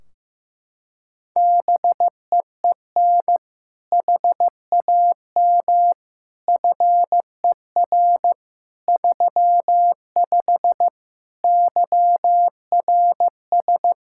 Morse code, heard as a keyed tone, about 16 wpm, 700 Hz. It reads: BEEN HAM FER 35 YRS
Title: Practice QSO - Exchange 3 - More Info - 15wpm